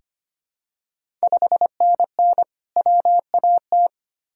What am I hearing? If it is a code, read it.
Morse code, heard as a keyed tone, about 25 wpm, 700 Hz. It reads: 5NN WAT
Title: Ontario QSO Party Exchange 25wpm